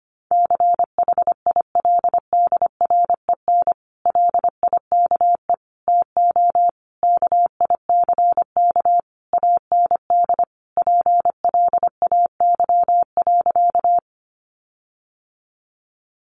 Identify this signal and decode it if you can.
Morse code, heard as a keyed tone, about 25 wpm, 700 Hz. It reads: CHILDREN LIKE TO KICK AND PLAY.
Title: Sentences from Top 1300 Words 25wpm